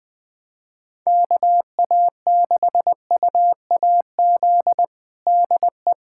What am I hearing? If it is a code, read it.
Morse code, heard as a keyed tone, about 20 wpm, 700 Hz. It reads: KA6UAZ DE
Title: USA Callsign DE Callsign - 6 Characters - 20wpm